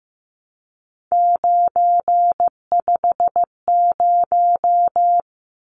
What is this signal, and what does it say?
Morse code, heard as a keyed tone, about 15 wpm, 700 Hz. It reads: 950
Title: Numbers - 3-Digit 15wpm